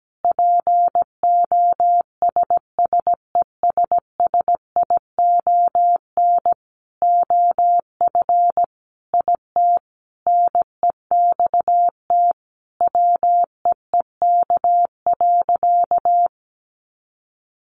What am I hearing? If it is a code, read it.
Morse code, heard as a keyed tone, about 17 wpm, 700 Hz. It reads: POSSESSION OF IT NEXT WEEK.
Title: Sentences from Top 1400 Words 17wpm